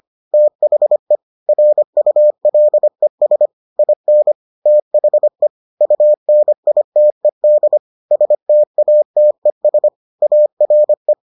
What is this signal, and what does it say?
Morse code, heard as a keyed tone, about 25 wpm, 600 Hz. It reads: THE RULES IN THE UNITED STATES ARE